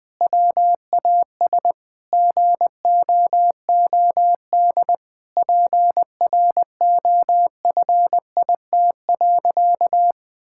Morse code, heard as a keyed tone, about 20 wpm, 700 Hz. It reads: WAS GOOD PROFIT.
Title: Sentences from Top 1600 Words 20wpm